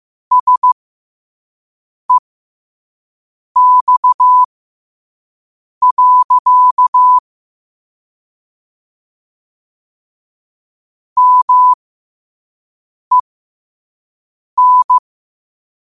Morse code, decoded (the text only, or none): SEX. MEN